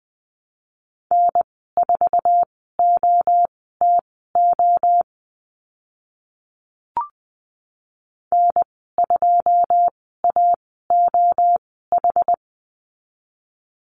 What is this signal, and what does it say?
Morse code, heard as a keyed tone, about 19 wpm, spaced as for 14 wpm, 700 Hz. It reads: N4OTO N2AOH
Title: 088 Lesson 15g - Number 5 - Call Signs 20wpm